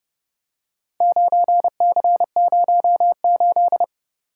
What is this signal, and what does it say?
Morse code, heard as a keyed tone, about 28 wpm, 700 Hz. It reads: 9C08